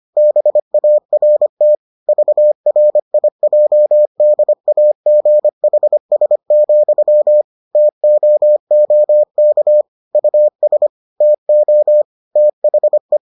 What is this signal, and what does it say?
Morse code, heard as a keyed tone, about 25 wpm, 600 Hz. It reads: BART VRIJDAGHS, TOOK US TO THE